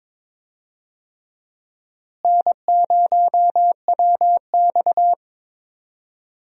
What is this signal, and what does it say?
Morse code, heard as a keyed tone, about 22 wpm, 700 Hz. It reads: N0WX
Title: USA Call Signs - 4 Characters - Repeated 3x - 22wpm